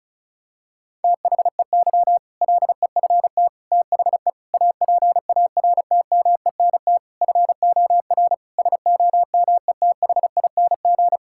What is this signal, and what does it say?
Morse code, heard as a keyed tone, about 36 wpm, 700 Hz. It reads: THEY LEFT THE APARTMENT FOR SOMETHING
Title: Sentences from Top 1000 Words - Review Part 6 - 35wpm